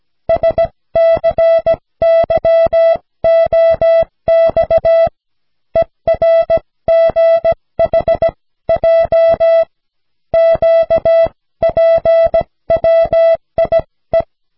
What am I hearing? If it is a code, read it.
Morse code, heard as a keyed tone, about 17 wpm, spaced as for 15 wpm, 660 Hz. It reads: SCYOX ERGHJ QPWIE
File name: cw / snelheid 4